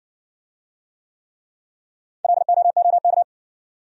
Morse code, heard as a keyed tone, about 60 wpm, 700 Hz. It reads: LOOK